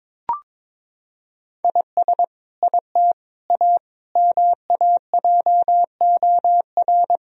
Morse code, heard as a keyed tone, about 22 wpm, 700 Hz. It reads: IS IT A MAJOR
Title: Sentences from Top 1000 Words - Review Part 4 - 22wpm